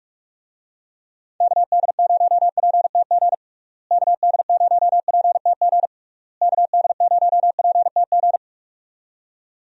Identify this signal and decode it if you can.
Morse code, heard as a keyed tone, about 46 wpm, 700 Hz. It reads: KD0PTG KD0PTG KD0PTG
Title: USA Call Signs - 6 Characters - Repeated 3x - 45wpm